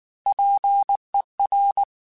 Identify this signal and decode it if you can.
Morse code, heard as a keyed tone, about 19 wpm, 800 Hz. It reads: PER
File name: Technical Subjects / Technical Subjects-mp3s / Reactance @18